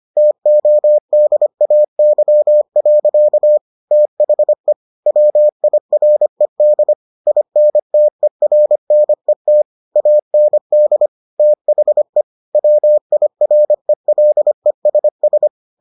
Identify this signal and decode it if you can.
Morse code, heard as a keyed tone, about 25 wpm, 600 Hz. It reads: TODAY. THE WIRED INTERNET AND THE WIRELESS